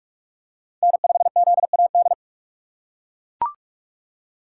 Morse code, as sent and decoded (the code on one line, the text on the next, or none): -. ..... --.. .- -..
N5ZAD